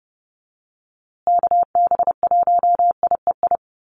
Morse code, heard as a keyed tone, about 30 wpm, 700 Hz. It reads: KB1IEI
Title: USA Call Signs Repeated 2x 30wpm